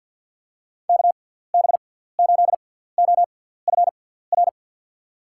Morse code, heard as a keyed tone, about 54 wpm, spaced as for 17 wpm, 700 Hz. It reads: KB8OFR
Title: USA Call Signs - 6 Characters - 6x Character Spacing - 50wpm